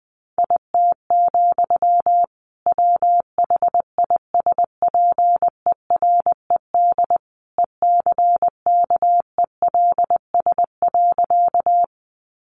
Morse code, heard as a keyed tone, about 20 wpm, 700 Hz. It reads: IT, WHISPERED ECKELS.
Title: A Sound Of Thunder 20wpm